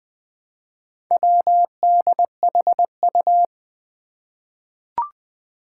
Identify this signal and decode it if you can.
Morse code, heard as a keyed tone, about 20 wpm, 700 Hz. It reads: WDHU
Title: Four Challenging Letter-Number Combinations 20wpm